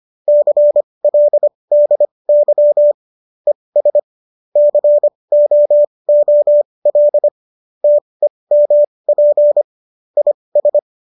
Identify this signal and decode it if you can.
Morse code, heard as a keyed tone, about 25 wpm, spaced as for 20 wpm, 590 Hz. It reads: CLDY ES COOL TEMP IS